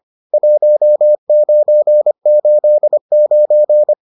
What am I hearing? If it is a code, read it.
Morse code, heard as a keyed tone, about 25 wpm, 600 Hz. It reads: 1989